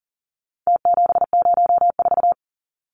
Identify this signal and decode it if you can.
Morse code, heard as a keyed tone, about 40 wpm, 700 Hz. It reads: T704
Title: Numbers - Contest Style - 4-Digit 40wpm